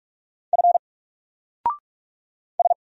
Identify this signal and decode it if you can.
Morse code, heard as a keyed tone, about 44 wpm, 700 Hz. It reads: F S
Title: Single Letter-Number Alt 45wpm